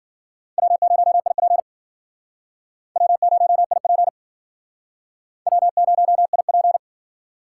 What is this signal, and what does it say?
Morse code, heard as a keyed tone, about 54 wpm, 700 Hz. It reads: W0IP W0IP W0IP